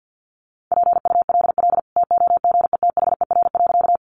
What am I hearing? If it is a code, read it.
Morse code, heard as a keyed tone, about 50 wpm, 700 Hz. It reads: PULL TOGETHER.
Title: Sentences from Top 400 Words 50wpm